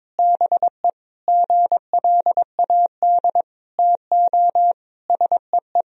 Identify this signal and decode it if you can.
Morse code, heard as a keyed tone, about 22 wpm, 700 Hz. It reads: BE GLAD TO SEE
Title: Sentences from Top 1000 Words - Review Part 2 - 22wpm